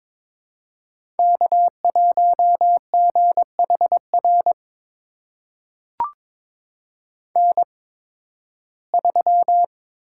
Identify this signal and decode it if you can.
Morse code, heard as a keyed tone, about 22 wpm, spaced as for 14 wpm, 700 Hz. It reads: K1GHR N3
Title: USA Call Signs - 5 Characters - 8x Character Spacing - 22wpm